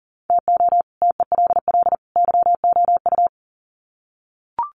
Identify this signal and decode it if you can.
Morse code, heard as a keyed tone, about 40 wpm, 700 Hz. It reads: TO TELL YOU
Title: Sets of 3 Words 40wpm